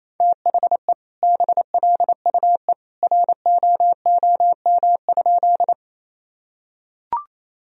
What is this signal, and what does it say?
Morse code, heard as a keyed tone, about 28 wpm, 700 Hz. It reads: THE BLUE ROOM?
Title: Sentences from Top 1000 Words - Review Part 3 - 28wpm